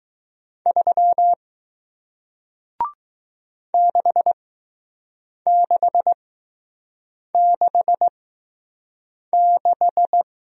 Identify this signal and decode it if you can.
Morse code, heard as a keyed tone, about 20 wpm, spaced as for 6 wpm, 700 Hz. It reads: 3 6666